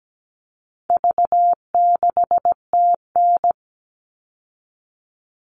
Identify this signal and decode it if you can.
Morse code, heard as a keyed tone, about 17 wpm, 700 Hz. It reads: V6TN
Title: All Cut Numbers - 4-Digit 17wpm